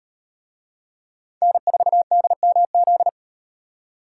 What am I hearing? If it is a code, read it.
Morse code, heard as a keyed tone, about 38 wpm, 700 Hz. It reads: N4DMZ